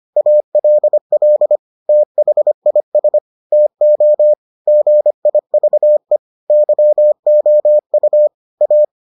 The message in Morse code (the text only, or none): ALL THIS TO GIVE YOU A